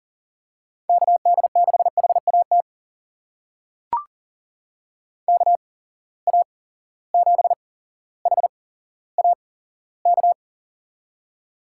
Morse code, heard as a keyed tone, about 40 wpm, spaced as for 17 wpm, 700 Hz. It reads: KD6HAT KA7HAK
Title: USA Call Signs - 6 Characters - 8x Character Spacing - 40wpm